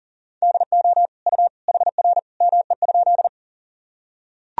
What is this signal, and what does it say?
Morse code, heard as a keyed tone, about 40 wpm, 700 Hz. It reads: DO U HR ME?
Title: Sentences from Top 300 Words and Abbreviations 40wpm